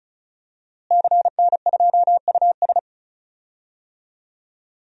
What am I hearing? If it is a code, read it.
Morse code, heard as a keyed tone, about 34 wpm, 700 Hz. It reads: CN2US
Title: DXCC-100 Call Signs - 5 Characters - Repeated 2x - 35wpm